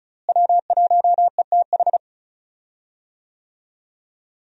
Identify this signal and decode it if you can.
Morse code, heard as a keyed tone, about 36 wpm, 700 Hz. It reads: W1ETH
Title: USA Call Signs - 5 Characters - Repeated 2x - 35wpm